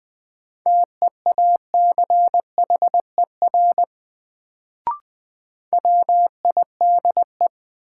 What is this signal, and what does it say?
Morse code, heard as a keyed tone, about 21 wpm, 700 Hz. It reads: TEACHER WIDE
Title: Sentences from Top 900 Words - New Word Drill - 20wpm